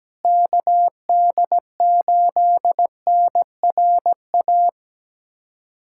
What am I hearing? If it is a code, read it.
Morse code, heard as a keyed tone, about 17 wpm, 700 Hz. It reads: KD8NRA
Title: USA Callsign DE Callsign 17wpm